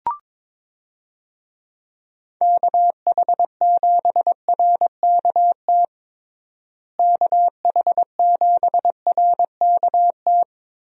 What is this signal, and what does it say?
Morse code, heard as a keyed tone, about 22 wpm, 700 Hz. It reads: KH7RKT KH7RKT
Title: USA Call Signs - 6 Characters - Repeated 3x - 22wpm